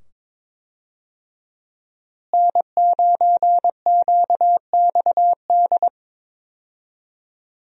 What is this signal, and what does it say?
Morse code, heard as a keyed tone, about 22 wpm, 700 Hz. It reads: N9QXD
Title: USA Call Signs - 5 Characters - Repeated 2x - 22wpm